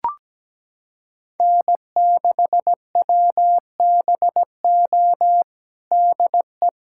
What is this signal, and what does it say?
Morse code, heard as a keyed tone, about 17 wpm, 700 Hz. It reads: N6WBO DE
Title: USA Callsign DE Callsign - 5 Characters - 17wpm